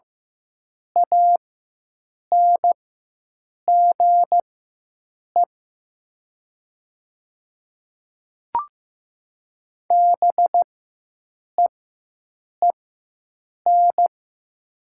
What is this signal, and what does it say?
Morse code, heard as a keyed tone, about 15 wpm, spaced as for 7 wpm, 700 Hz. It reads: ANGE BEEN
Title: 4x Character Spacing - Top 200 Words - 15wpm